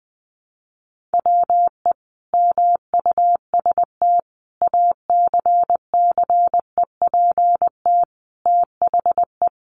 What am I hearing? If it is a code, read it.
Morse code, heard as a keyed tone, about 20 wpm, 700 Hz. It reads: WE MUST ACCEPT THE
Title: Sentences from Top 1300 Words 20wpm